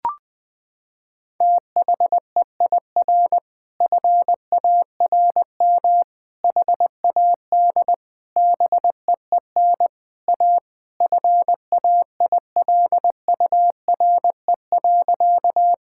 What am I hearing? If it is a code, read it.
Morse code, heard as a keyed tone, about 20 wpm, 700 Hz. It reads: THEIR FARM HAD BEEN A FAILURE.
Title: Sentences from Top 1000 Words - Review Part 6 - 20wpm